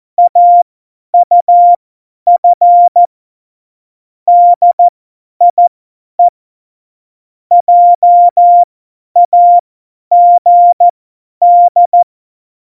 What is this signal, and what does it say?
Morse code, heard as a keyed tone, about 14 wpm, spaced as for 10 wpm, 700 Hz. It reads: AUF DIE JAGD
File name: Morsekurs / Witze / Witz 27 Juni 17, Tempo 50 F70